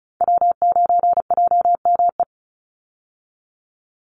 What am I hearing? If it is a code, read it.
Morse code, heard as a keyed tone, about 34 wpm, 700 Hz. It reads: W9JME
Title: USA Call Signs 35wpm